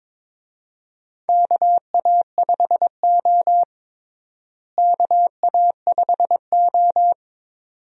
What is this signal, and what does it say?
Morse code, heard as a keyed tone, about 22 wpm, 700 Hz. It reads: KA5O KA5O
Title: USA Call Signs - 4 Characters - Repeated 2x - 22wpm